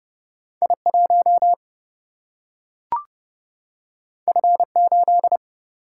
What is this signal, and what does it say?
Morse code, heard as a keyed tone, about 30 wpm, 700 Hz. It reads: I1 F8
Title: Single-Letter Single-Number 30wpm